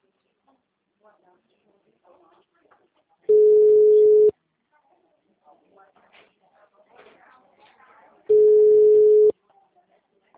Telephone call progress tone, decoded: ringback tone

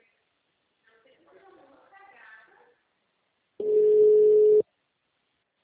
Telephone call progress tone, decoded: ringback tone

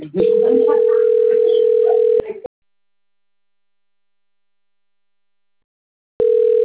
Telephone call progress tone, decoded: ringback tone

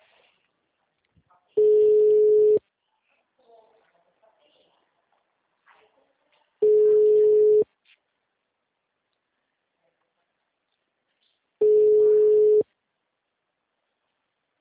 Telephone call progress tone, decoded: ringback tone

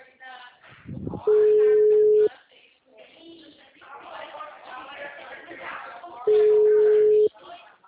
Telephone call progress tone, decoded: ringback tone

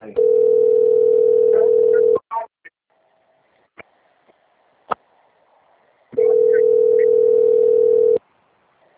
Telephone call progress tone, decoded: ringback tone

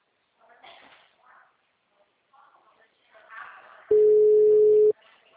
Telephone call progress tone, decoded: ringback tone

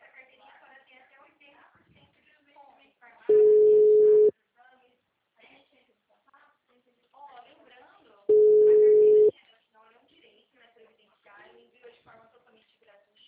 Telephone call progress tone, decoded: ringback tone

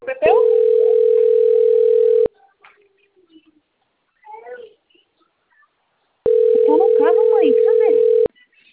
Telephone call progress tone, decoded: ringback tone

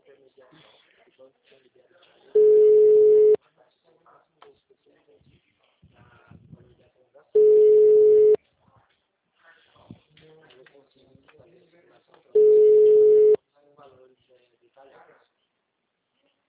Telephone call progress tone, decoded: ringback tone

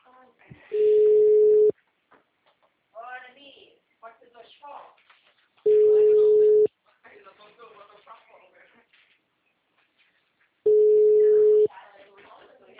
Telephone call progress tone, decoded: ringback tone

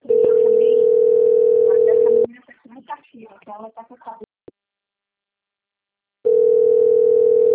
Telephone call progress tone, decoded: ringback tone

